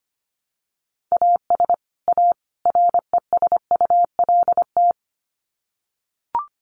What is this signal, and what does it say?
Morse code, heard as a keyed tone, about 25 wpm, 700 Hz. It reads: AS A RESULT
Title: Sets of 3 Words 25wpm